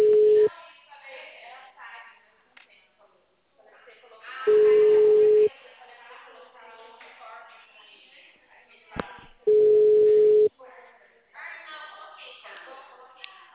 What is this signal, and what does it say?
Ringback tone (Europe, 425 Hz), with speech.
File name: out- -2044- - - .12965